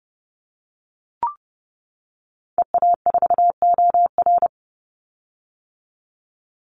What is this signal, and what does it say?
Morse code, heard as a keyed tone, about 30 wpm, 700 Hz. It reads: EA4OR
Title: Call Signs - 30wpm